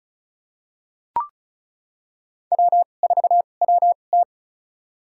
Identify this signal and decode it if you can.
Morse code, heard as a keyed tone, about 36 wpm, spaced as for 25 wpm, 700 Hz. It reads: W4WT
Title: USA Call Signs - 4 Characters - 2x Character Spacing - 35wpm